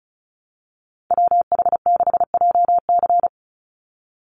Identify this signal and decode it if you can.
Morse code, heard as a keyed tone, about 34 wpm, 700 Hz. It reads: WH6JC
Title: USA Call Signs 35wpm